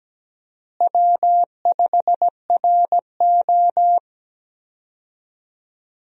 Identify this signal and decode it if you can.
Morse code, heard as a keyed tone, about 17 wpm, 700 Hz. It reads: W5RO
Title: USA Call Signs - 4 Characters - Repeated 2x - 17wpm